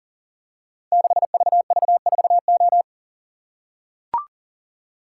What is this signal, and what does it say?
Morse code, heard as a keyed tone, about 42 wpm, 700 Hz. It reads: 6VV4O
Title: Code Groups - 5 Letter:Numbers - 40wpm